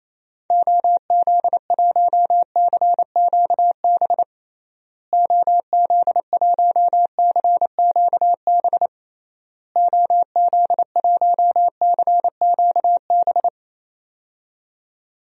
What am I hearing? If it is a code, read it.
Morse code, heard as a keyed tone, about 28 wpm, 700 Hz. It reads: OZ1CQB OZ1CQB OZ1CQB
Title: DXCC-100 Call Signs - 6 Characters - Repeated 3x - 28wpm